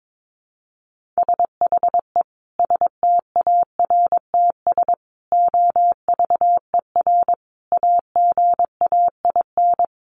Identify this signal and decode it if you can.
Morse code, heard as a keyed tone, about 22 wpm, 700 Hz. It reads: SHE STARTS OVER AGAIN
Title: Easy Phrases 22wpm